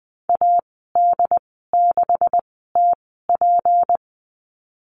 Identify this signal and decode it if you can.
Morse code, heard as a keyed tone, about 20 wpm, spaced as for 15 wpm, 700 Hz. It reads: AD6TP
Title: 167 Lesson 27g - Number 7 - Call Signs 20wpm